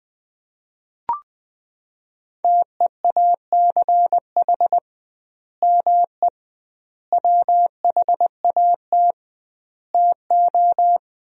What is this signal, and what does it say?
Morse code, heard as a keyed tone, about 20 wpm, 700 Hz. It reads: TEACH ME WHAT TO
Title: 2x Word-Spacing - Sentences from Top 500 Words - 20wpm